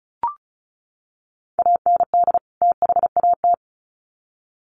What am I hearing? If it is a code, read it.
Morse code, heard as a keyed tone, about 34 wpm, 700 Hz. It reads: AND THAT
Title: Sets of 2 Words 35wpm